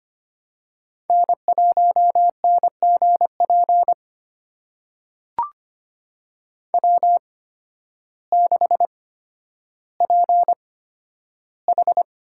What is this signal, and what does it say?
Morse code, heard as a keyed tone, about 25 wpm, spaced as for 12 wpm, 700 Hz. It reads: N1NGP W6PH
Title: USA Call Signs - 8x Character Spacing - 25wpm